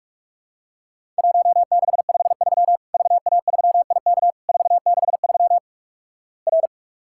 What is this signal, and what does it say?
Morse code, heard as a keyed tone, about 46 wpm, 700 Hz. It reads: 1653 VA3IK 463 R
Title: Worked All Europe DX Contest - QTC Exchange - 45wpm